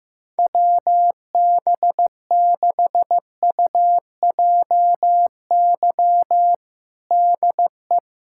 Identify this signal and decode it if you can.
Morse code, heard as a keyed tone, about 15 wpm, 700 Hz. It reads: WB6UJY DE
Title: USA Callsign DE Callsign 15wpm